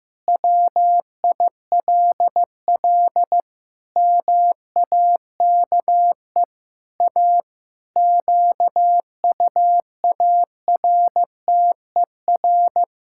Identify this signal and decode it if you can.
Morse code, heard as a keyed tone, about 15 wpm, 700 Hz. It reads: WILL MAKE A QUARTER
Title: Pangram Challenge v2 - 15wpm